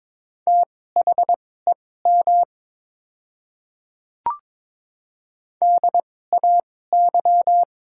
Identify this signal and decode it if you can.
Morse code, heard as a keyed tone, about 22 wpm, spaced as for 16 wpm, 700 Hz. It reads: THEM DAY
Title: 2x Character Spacing - Top 100 Words - 22wpm